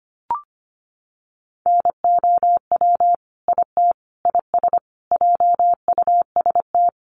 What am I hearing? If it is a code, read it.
Morse code, heard as a keyed tone, about 25 wpm, 700 Hz. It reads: NOW IT IS JUST
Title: Sentences from Top 1300 Words 25wpm